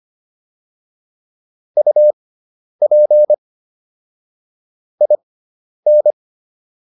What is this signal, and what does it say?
Morse code, heard as a keyed tone, about 25 wpm, spaced as for 10 wpm, 600 Hz. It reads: UP IN